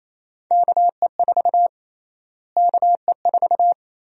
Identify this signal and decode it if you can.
Morse code, heard as a keyed tone, about 28 wpm, 700 Hz. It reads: KE4 KE4
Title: USA Call Sign Prefixes - Repeated 2x - 28wpm